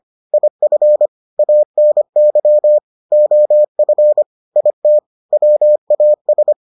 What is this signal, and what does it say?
Morse code, heard as a keyed tone, about 25 wpm, 600 Hz. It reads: IF ANY OF IT WAS